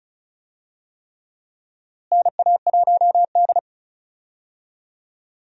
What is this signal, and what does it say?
Morse code, heard as a keyed tone, about 36 wpm, 700 Hz. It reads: NA1D
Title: USA Call Signs - 4 Characters - Repeated 3x - 35wpm